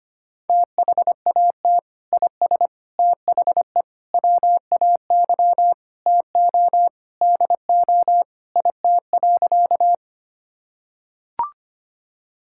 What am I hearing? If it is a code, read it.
Morse code, heard as a keyed tone, about 25 wpm, 700 Hz. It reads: THAT IS THE WAY TO DO IT.
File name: practice / 25wpm / Sentences with 3-Word Phrases - 25wpm